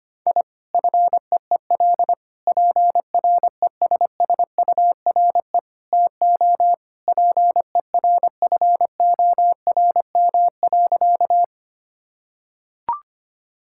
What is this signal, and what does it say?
Morse code, heard as a keyed tone, about 25 wpm, 700 Hz. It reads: I FEEL PRESSURE TO PERFORM.
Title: Sentences from Top 1000 Words - Review Part 6 - 25wpm